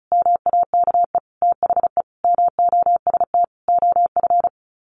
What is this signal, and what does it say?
Morse code, heard as a keyed tone, about 34 wpm, 700 Hz. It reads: MAKE THE MOST OF
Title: Sets of 5 Words Encore2 35wpm